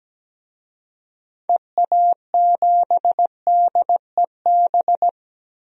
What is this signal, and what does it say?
Morse code, heard as a keyed tone, about 17 wpm, 700 Hz. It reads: EA7DEB
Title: DXCC-100 Call Signs - 6 Characters - Repeated 2x - 17wpm